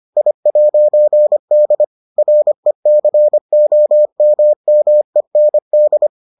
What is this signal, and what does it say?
Morse code, heard as a keyed tone, about 25 wpm, 600 Hz. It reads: I'D RECOMMEND